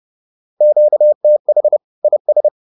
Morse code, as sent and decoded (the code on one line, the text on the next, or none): --.- - .... / .. ...
QTH IS